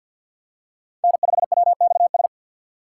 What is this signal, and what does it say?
Morse code, heard as a keyed tone, about 53 wpm, 700 Hz. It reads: N5WXS
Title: USA Call Signs - 5 Characters - 4x Character Spacing - 50wpm